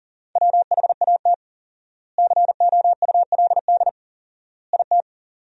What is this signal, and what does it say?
Morse code, heard as a keyed tone, about 40 wpm, 700 Hz. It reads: WHAT COULD IT
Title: 4x Word-Spacing - Sentences from Top 100 Words - 40wpm